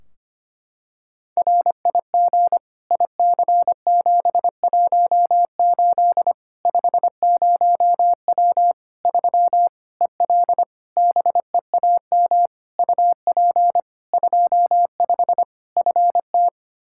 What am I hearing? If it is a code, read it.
Morse code, heard as a keyed tone, about 25 wpm, 700 Hz. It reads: RIG IC718 50W 3 EL BEAM UP 25 FT
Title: Practice QSO - Exchange 2 - Info - 25wpm